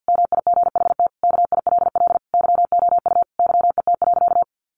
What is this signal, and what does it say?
Morse code, heard as a keyed tone, about 50 wpm, 700 Hz. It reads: MIGHT KILL YOU YET.
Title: A Sound Of Thunder 50wpm